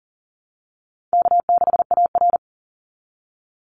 Morse code, heard as a keyed tone, about 38 wpm, 700 Hz. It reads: K6AR
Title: Call Signs - 40wpm